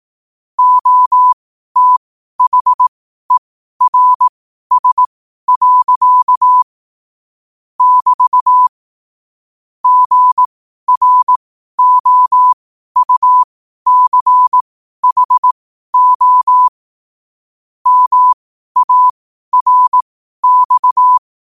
Morse code, heard as a keyed tone, about 18 wpm, spaced as for 12 wpm, 1000 Hz. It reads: OTHERS. <BT> GROUCHO MARX
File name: qotdfiles / 12wpm